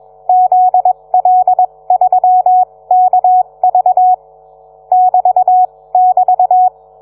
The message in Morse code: ZL3KV <BT><BT>